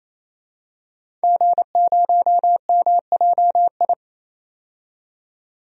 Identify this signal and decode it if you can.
Morse code, heard as a keyed tone, about 28 wpm, 700 Hz. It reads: G0MJI
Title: DXCC-100 Call Signs - 5 Characters - 28wpm